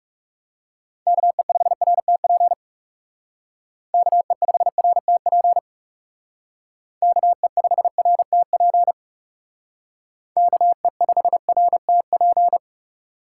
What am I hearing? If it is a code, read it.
Morse code, heard as a keyed tone, about 38 wpm, 700 Hz. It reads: KE5RTP KE5RTP KE5RTP KE5RTP